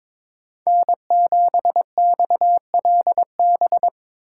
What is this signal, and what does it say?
Morse code, heard as a keyed tone, about 22 wpm, 700 Hz. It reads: N7XLB
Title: USA Call Signs - 5 Characters - 3x Character Spacing - 22wpm